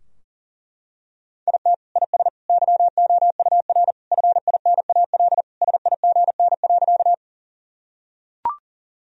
Morse code, heard as a keyed tone, about 40 wpm, 700 Hz. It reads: IT IS YOUR FINAL SIGN.
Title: Sentences from Top 1000 Words - Review Part 2 - 40wpm